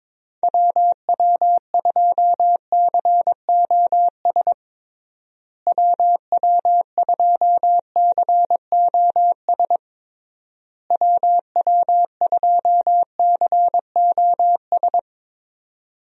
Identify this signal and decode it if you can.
Morse code, heard as a keyed tone, about 22 wpm, 700 Hz. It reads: WW2COS WW2COS WW2COS
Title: USA Call Signs - 6 Characters - Repeated 3x - 22wpm